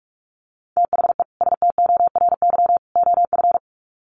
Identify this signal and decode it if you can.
Morse code, heard as a keyed tone, about 44 wpm, 700 Hz. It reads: THE STORY OF
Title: Sets of 3 Words 45wpm